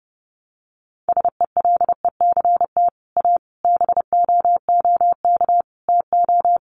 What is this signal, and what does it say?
Morse code, heard as a keyed tone, about 30 wpm, 700 Hz. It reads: SELECT A BOOK TO
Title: Sentences from Top 1300 Words 30wpm